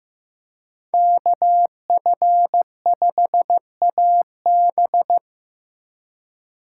Morse code, heard as a keyed tone, about 15 wpm, 700 Hz. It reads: KF5AB
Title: USA Call Signs - 2x Character Spacing - 15wpm